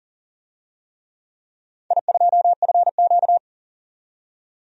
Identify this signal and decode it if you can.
Morse code, heard as a keyed tone, about 40 wpm, 700 Hz. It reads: I2FQ
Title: DXCC-100 Call Signs - 4 Characters - 40wpm